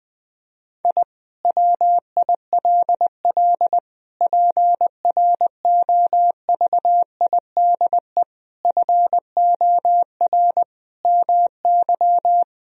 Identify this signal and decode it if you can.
Morse code, heard as a keyed tone, about 20 wpm, 700 Hz. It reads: I WILL PROVIDE FOR MY
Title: Sentences from Top 1000 Words - Review Part 3 - 20wpm